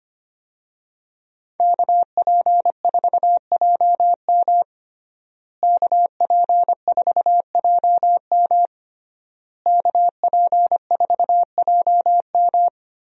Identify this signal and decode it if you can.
Morse code, heard as a keyed tone, about 25 wpm, 700 Hz. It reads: KP4JM KP4JM KP4JM
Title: DXCC-100 Call Signs - 5 Characters - Repeated 3x - 25wpm